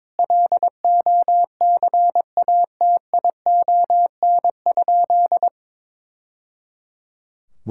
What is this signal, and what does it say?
Morse code, heard as a keyed tone, about 22 wpm, 700 Hz. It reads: LOCATION?
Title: Sentences from Top 1000 Words - Review Part 6 - 22wpm